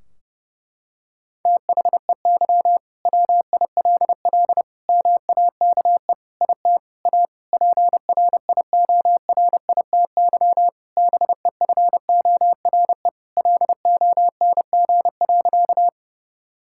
Morse code, heard as a keyed tone, about 32 wpm, 700 Hz. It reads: THEY WILL MAKE IT A PRIORITY BEFORE LONG.
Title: Sentences from Top 1600 Words 30wpm